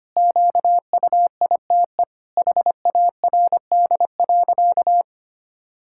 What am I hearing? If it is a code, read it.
Morse code, heard as a keyed tone, about 25 wpm, 700 Hz. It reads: QUITE HARD.
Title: Pangram Challenge v2 - 25wpm